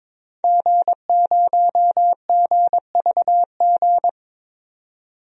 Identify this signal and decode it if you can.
Morse code, heard as a keyed tone, about 22 wpm, 700 Hz. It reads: G0GVG
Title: DXCC-100 Call Signs - 5 Characters - 22wpm